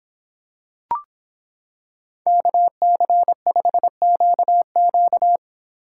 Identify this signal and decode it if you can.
Morse code, heard as a keyed tone, about 25 wpm, 700 Hz. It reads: KC5QQ